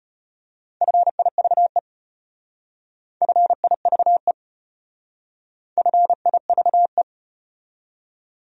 Morse code, heard as a keyed tone, about 33 wpm, 700 Hz. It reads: FIVE FIVE FIVE